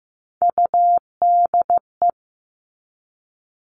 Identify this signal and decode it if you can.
Morse code, heard as a keyed tone, about 15 wpm, 700 Hz. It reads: UDE
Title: All Cut Numbers - 3-Digit 15wpm